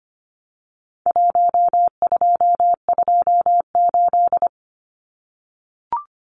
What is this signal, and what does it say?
Morse code, heard as a keyed tone, about 25 wpm, 700 Hz. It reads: 1228
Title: Numbers - Contest Style - 4-Digit 25wpm